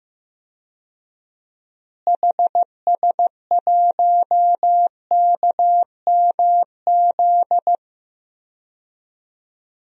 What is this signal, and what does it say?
Morse code, heard as a keyed tone, about 15 wpm, 700 Hz. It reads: HS1KMZ
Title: DXCC-100 Call Signs - 6 Characters - Repeated 2x - 15wpm